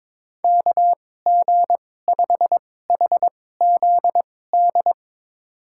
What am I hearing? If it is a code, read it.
Morse code, heard as a keyed tone, about 22 wpm, spaced as for 16 wpm, 700 Hz. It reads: KG5HZD
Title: USA Call Signs - 6 Characters - 2x Character Spacing - 22wpm